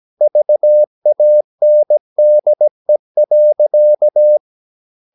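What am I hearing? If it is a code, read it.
Morse code, heard as a keyed tone, about 17 wpm, 600 Hz. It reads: VANDE.
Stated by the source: Slow (QRS) Morse practice at 17 wpm, from CW med Gnister.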